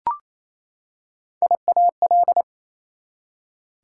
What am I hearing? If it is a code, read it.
Morse code, heard as a keyed tone, about 28 wpm, 700 Hz. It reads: IAL
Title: Prefixes and Suffixes 28wpm